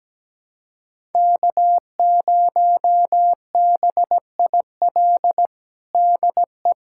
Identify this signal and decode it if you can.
Morse code, heard as a keyed tone, about 17 wpm, 700 Hz. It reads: K0BIL DE
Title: USA Callsign DE Callsign - 5 Characters - 17wpm